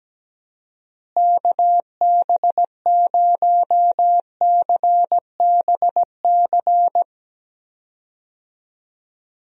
Morse code, heard as a keyed tone, about 17 wpm, 700 Hz. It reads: KB0CBC
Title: USA Call Signs - 6 Characters - Repeated 2x - 17wpm